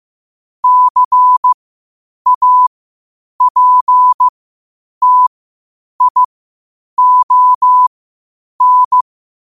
Morse code, heard as a keyed tone, about 15 wpm, spaced as for 8 wpm, 1000 Hz. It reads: CAPTION
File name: qotdfiles / 8wpm